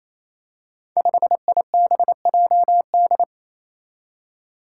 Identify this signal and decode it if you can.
Morse code, heard as a keyed tone, about 27 wpm, 700 Hz. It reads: 5IBJD
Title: Five Challenging Letter-Number Combinations 28wpm